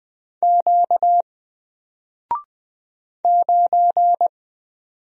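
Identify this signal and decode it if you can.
Morse code, heard as a keyed tone, about 20 wpm, 700 Hz. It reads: Q 9
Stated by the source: Single letter and number Morse drill at 20 wpm, from N4LCD.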